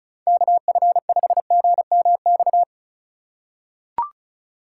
Morse code, heard as a keyed tone, about 36 wpm, 700 Hz. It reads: KF5GMX
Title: USA Call Signs - 6 Characters - 3x Character Spacing - 35wpm